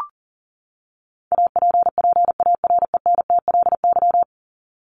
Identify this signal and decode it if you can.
Morse code, heard as a keyed tone, about 40 wpm, 700 Hz. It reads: APPARENTLY
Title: LY Suffix - Top 100 Words with Base Words 40wpm